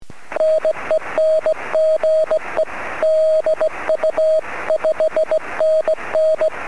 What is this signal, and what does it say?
Morse code, heard as a keyed tone, about 18 wpm, 630 Hz. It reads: NEN GE DU 5NN
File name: sm7lcb / 01-sm0err 23cm